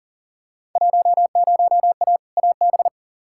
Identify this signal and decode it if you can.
Morse code, heard as a keyed tone, about 40 wpm, 700 Hz. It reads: 10A AB
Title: Field Day Exchanges Encore - 40wpm